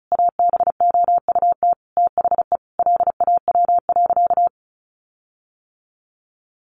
Morse code, heard as a keyed tone, about 34 wpm, 700 Hz. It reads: ABOUT THE LAW.
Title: Sentences from Top 1400 Words 35wpm